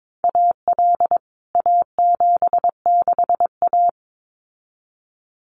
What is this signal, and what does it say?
Morse code, heard as a keyed tone, about 22 wpm, 700 Hz. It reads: AL A76A
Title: CWT Exchange 22wpm